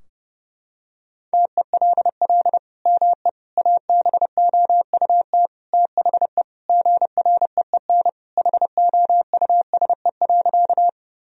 Morse code, heard as a keyed tone, about 32 wpm, 700 Hz. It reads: TELL ME ABOUT THE GREEN HOUSE.
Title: Sentences from Top 1000 Words - Review Part 2 - 30wpm